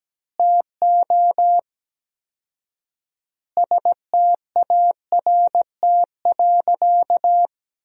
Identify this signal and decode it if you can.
Morse code, heard as a keyed tone, about 17 wpm, 700 Hz. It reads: TO START.
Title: 4x Word-Spacing - Sentences from Top 300 Words - 17wpm